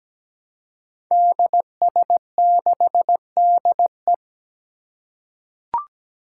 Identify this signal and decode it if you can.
Morse code, heard as a keyed tone, about 17 wpm, 700 Hz. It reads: DS6DE
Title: Five Challenging Letter-Number Combinations 17wpm